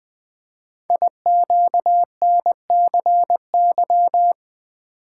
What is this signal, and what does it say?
Morse code, heard as a keyed tone, about 20 wpm, 700 Hz. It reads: IQNCY
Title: Code Groups - 5 Letters - 20wpm